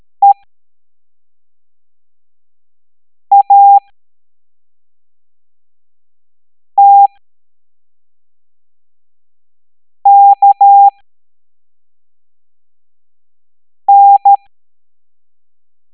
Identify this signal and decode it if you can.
Morse code, heard as a keyed tone, about 13 wpm, spaced as for 3 wpm, 800 Hz. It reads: EATKN
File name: hamdata / L01-EX02-ETANTK